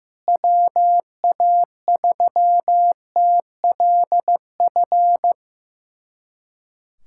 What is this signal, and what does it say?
Morse code, heard as a keyed tone, about 16 wpm, 700 Hz. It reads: WA3TLF
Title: USA Callsign DE Callsign 15wpm